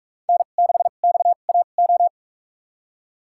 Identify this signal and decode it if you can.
Morse code, heard as a keyed tone, about 43 wpm, spaced as for 32 wpm, 700 Hz. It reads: N6XAO
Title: USA Call Signs - 5 Characters - 2x Character Spacing - 45wpm